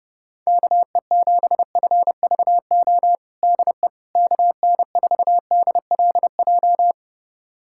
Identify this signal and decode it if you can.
Morse code, heard as a keyed tone, about 32 wpm, 700 Hz. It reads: KE7FVO DE KN4DLJ
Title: USA Callsign DE Callsign - 6 Characters - 30wpm